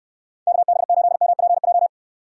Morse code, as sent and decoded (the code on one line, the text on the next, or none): -.- -... ---.. -- .--. .---
KB8MPJ